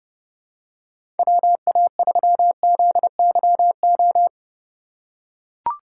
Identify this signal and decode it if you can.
Morse code, heard as a keyed tone, about 31 wpm, 700 Hz. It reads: WA3ZYO
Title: USA Call Signs - 6 Characters - 4x Character Spacing - 30wpm